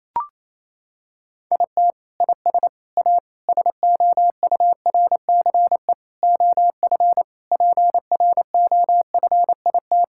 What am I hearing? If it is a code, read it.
Morse code, heard as a keyed tone, about 28 wpm, 700 Hz. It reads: IT IS A SOURCE OF PROFIT
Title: Sentences from Top 1000 Words - Review Part 5 - 28wpm